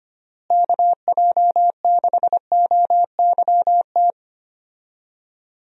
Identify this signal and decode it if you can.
Morse code, heard as a keyed tone, about 25 wpm, 700 Hz. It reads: KJ6OYT
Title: USA Call Signs - 6 Characters - Repeated 2x - 25wpm